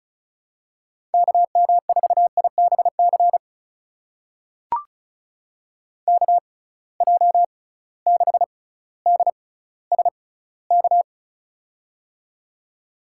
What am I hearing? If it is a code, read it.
Morse code, heard as a keyed tone, about 36 wpm, spaced as for 18 wpm, 700 Hz. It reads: KM4IBC KJ6DSK
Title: USA Call Signs - 6 Characters - 6x Character Spacing - 35wpm